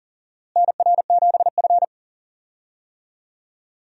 Morse code, heard as a keyed tone, about 40 wpm, 700 Hz. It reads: NR7F
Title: USA Call Signs - 4 Characters - Repeated 2x - 40wpm